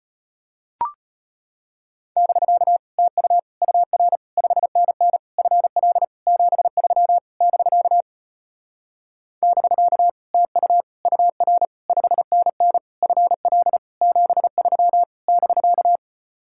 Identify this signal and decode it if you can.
Morse code, heard as a keyed tone, about 33 wpm, 700 Hz. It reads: <BK> TU UR 5NN FL 73 <BK> <BK> TU UR 5NN FL 73 <BK>